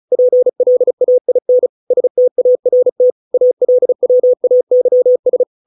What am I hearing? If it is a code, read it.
Morse code, heard as a keyed tone, about 34 wpm, 500 Hz. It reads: PLAIN START ALWAYS